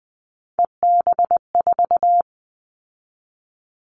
Morse code, heard as a keyed tone, about 20 wpm, 700 Hz. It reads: EB4
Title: Callsign Prefixes for Top 25 DXCC - 20wpm